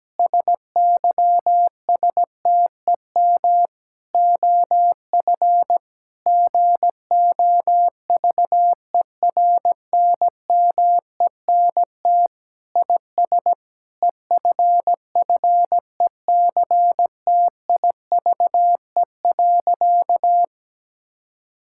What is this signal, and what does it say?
Morse code, heard as a keyed tone, about 17 wpm, 700 Hz. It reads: SYSTEM OF GOVERNMENT IS EFFECTIVE.
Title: Sentences from Top 1000 Words - Review Part 6 - 17wpm